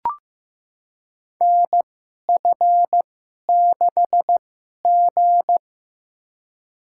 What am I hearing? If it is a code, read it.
Morse code, heard as a keyed tone, about 15 wpm, spaced as for 11 wpm, 700 Hz. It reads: NF6G
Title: USA Call Signs - 4 Characters - 2x Character Spacing - 15wpm